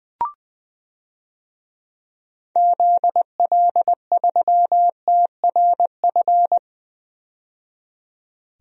Morse code, heard as a keyed tone, about 20 wpm, 700 Hz. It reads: ZL3TRF
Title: DXCC-100 Call Signs - 6 Characters - 20wpm